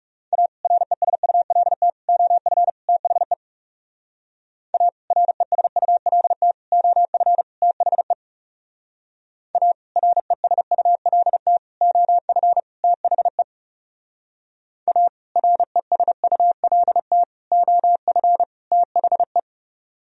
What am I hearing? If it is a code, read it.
Morse code, heard as a keyed tone, about 39 wpm, 700 Hz. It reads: A RESULT OF THE A RESULT OF THE A RESULT OF THE A RESULT OF THE